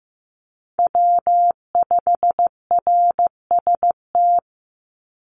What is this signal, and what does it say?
Morse code, heard as a keyed tone, about 15 wpm, 700 Hz. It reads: W5RST
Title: NA SOTA Chaser Callsigns 15wpm